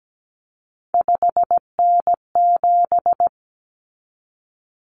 Morse code, heard as a keyed tone, about 17 wpm, 700 Hz. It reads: 5N7
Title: Common Cut Numbers - 3-Digit 17wpm